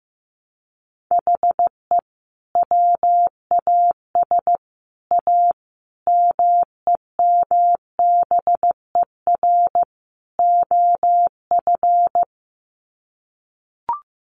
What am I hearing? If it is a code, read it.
Morse code, heard as a keyed tone, about 15 wpm, 700 Hz. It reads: HE WAS A MEMBER OF
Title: Sets of 5 Words Encore2 15wpm